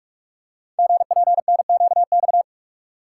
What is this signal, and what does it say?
Morse code, heard as a keyed tone, about 44 wpm, 700 Hz. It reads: GPNQX
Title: Code Groups - 5 Letter:Numbers - 45wpm